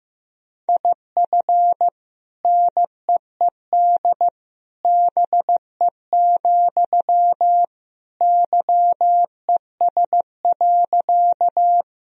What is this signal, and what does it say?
Morse code, heard as a keyed tone, about 16 wpm, 700 Hz. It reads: IF NEED BE, YES.
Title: Sentences from Top 1000 Words - Review Part 1 - 15wpm